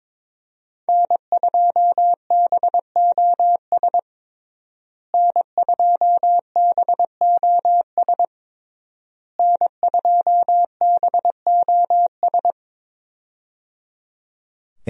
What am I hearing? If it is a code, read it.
Morse code, heard as a keyed tone, about 22 wpm, 700 Hz. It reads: N2BOS N2BOS N2BOS
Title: USA Call Signs - 5 Characters - Repeated 3x - 22wpm